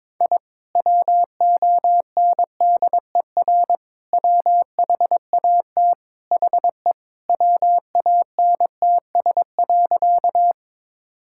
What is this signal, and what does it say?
Morse code, heard as a keyed tone, about 21 wpm, 700 Hz. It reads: I WONDER WHAT HE WANTS.
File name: practice / 22wpm / Sentences with 2-Word Phrases - 22wpm